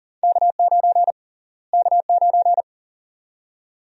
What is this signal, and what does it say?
Morse code, heard as a keyed tone, about 40 wpm, 700 Hz. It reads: K9 K9
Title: USA Call Sign Prefixes - Repeated 2x - 40wpm